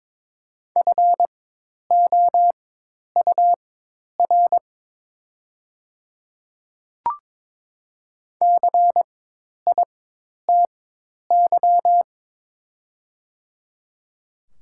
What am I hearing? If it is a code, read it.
Morse code, heard as a keyed tone, about 22 wpm, spaced as for 10 wpm, 690 Hz. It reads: FOUR CITY
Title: 4x Character Spacing - Top 300 Words - 22wpm